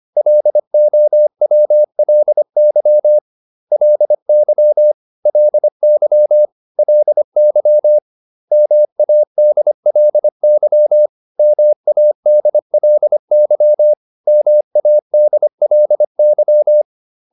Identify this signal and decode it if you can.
Morse code, heard as a keyed tone, about 25 wpm, 600 Hz. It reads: LOWLY LY LY LY MADLY MADLY MADLY